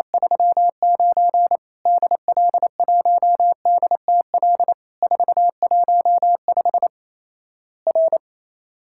Morse code, heard as a keyed tone, about 28 wpm, 700 Hz. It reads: 39 DL1DTL 415 R
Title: Worked All Europe DX Contest - QTC Exchange - 28wpm